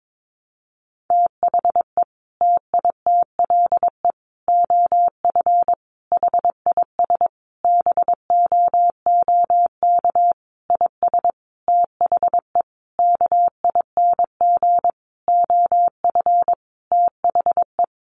Morse code, heard as a keyed tone, about 22 wpm, 700 Hz. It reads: THE TITLE OF HIS BOOK IS THE KING OF THE
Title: Sentences from Top 1200 Words 22wpm